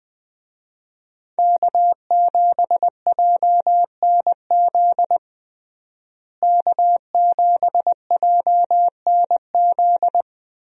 Morse code, heard as a keyed tone, about 20 wpm, 700 Hz. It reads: K7JNZ K7JNZ
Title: USA Call Signs - 5 Characters - Repeated 3x - 20wpm